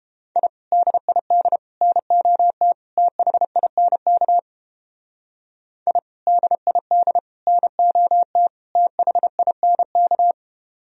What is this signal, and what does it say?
Morse code, heard as a keyed tone, about 29 wpm, 700 Hz. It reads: I DID NOT THINK I DID NOT THINK